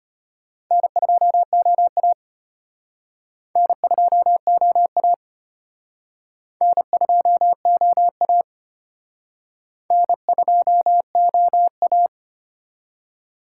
Morse code, heard as a keyed tone, about 31 wpm, 700 Hz. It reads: N2OA N2OA N2OA N2OA